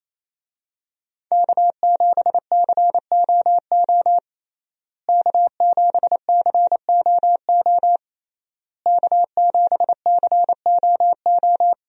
Morse code, heard as a keyed tone, about 27 wpm, 700 Hz. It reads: K7COO K7COO K7COO
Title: USA Call Signs - 5 Characters - Repeated 3x - 28wpm